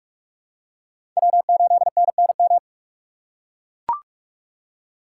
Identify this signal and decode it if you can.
Morse code, heard as a keyed tone, about 41 wpm, 700 Hz. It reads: W8NNM
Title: USA Call Signs - 6x Character Spacing - 45wpm